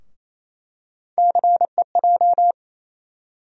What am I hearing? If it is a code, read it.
Morse code, heard as a keyed tone, about 28 wpm, 700 Hz. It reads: CEJ
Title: USA Call Sign Postfixes - 3 Letters 28wpm